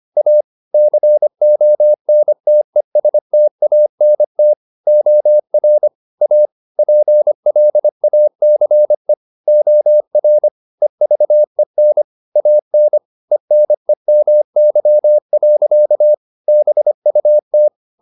Morse code, heard as a keyed tone, about 25 wpm, 600 Hz. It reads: A CONTESTANT OR A PLACE OR EVEN AN ENEMY. BUT
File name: the old man and the sea part 1 25wpm